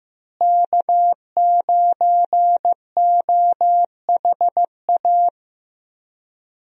Morse code, heard as a keyed tone, about 15 wpm, 700 Hz. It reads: K9OHA
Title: USA Call Signs - 5 Characters - Repeated 3x - 15wpm